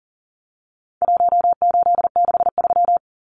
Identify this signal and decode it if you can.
Morse code, heard as a keyed tone, about 40 wpm, 700 Hz. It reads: 1863
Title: Numbers - Contest Style - 4-Digit 40wpm